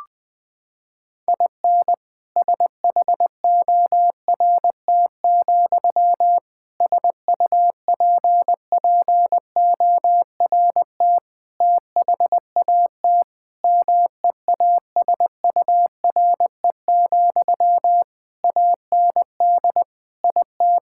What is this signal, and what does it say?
Morse code, heard as a keyed tone, about 21 wpm, 700 Hz. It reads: IN SHORT, SUPPORT THAT MEASURE, AND IT
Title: Sentences from Top 1000 Words - Review Part 5 - 20wpm